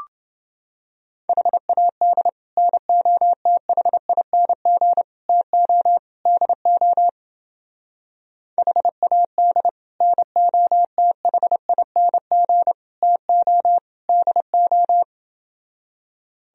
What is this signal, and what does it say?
Morse code, heard as a keyed tone, about 29 wpm, 700 Hz. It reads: HAD NOTHING TO DO HAD NOTHING TO DO